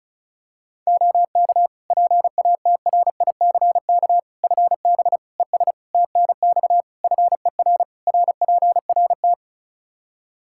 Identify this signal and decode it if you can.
Morse code, heard as a keyed tone, about 34 wpm, 700 Hz. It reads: OK PATRICK FB ES TNX FER RPRT
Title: Practice QSO - Exchange 2 - Info - 35wpm